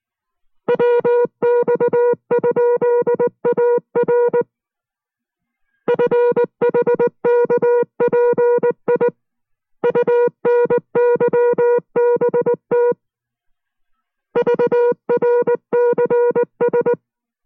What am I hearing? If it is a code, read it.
Morse code, heard as a keyed tone, about 19 wpm, 480 Hz. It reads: WX?AR FHKPI UNYBT VRCS